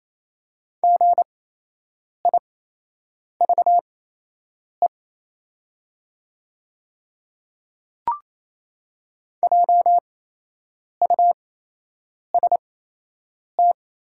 Morse code, heard as a keyed tone, about 27 wpm, spaced as for 8 wpm, 700 Hz. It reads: GIVE JUST
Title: 8x Character Spacing - Top 200 Words - 28wpm